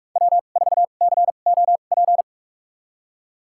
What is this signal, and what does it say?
Morse code, heard as a keyed tone, about 46 wpm, spaced as for 33 wpm, 700 Hz. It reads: W4COP
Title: USA Call Signs - 5 Characters - 2x Character Spacing - 45wpm